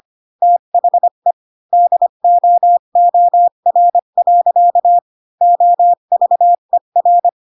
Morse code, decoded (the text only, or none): THE DOOR. OVER